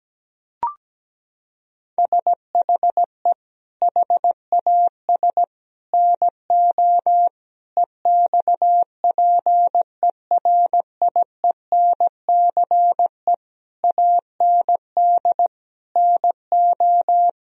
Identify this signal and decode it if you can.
Morse code, heard as a keyed tone, about 17 wpm, 700 Hz. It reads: SHE HAS NO EXPERIENCE AND NO
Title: Sentences from Top 1000 Words - Review Part 4 - 17wpm